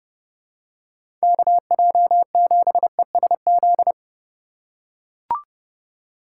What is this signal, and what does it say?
Morse code, heard as a keyed tone, about 32 wpm, 700 Hz. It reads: KJ7ESZ
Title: USA Call Signs - 6 Characters - 4x Character Spacing - 30wpm